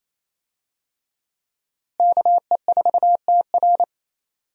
Morse code, heard as a keyed tone, about 28 wpm, 700 Hz. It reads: KE4TR
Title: USA Call Signs - 5 Characters - Repeated 3x - 28wpm